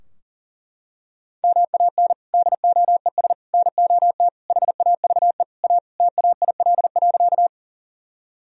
Morse code, heard as a keyed tone, about 40 wpm, 700 Hz. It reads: MAN DOES NOT HAVE A TAIL.
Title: Sentences from Top 1000 Words - Review Part 3 - 40wpm